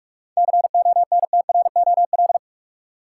Morse code, heard as a keyed tone, about 46 wpm, 700 Hz. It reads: CONTROL
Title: Seven Letter Words - 45wpm